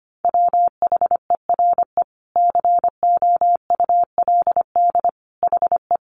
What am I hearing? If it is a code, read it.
Morse code, heard as a keyed tone, about 25 wpm, 700 Hz. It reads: WHERE COULD HE
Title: Sentences from Top 200 Words 25wpm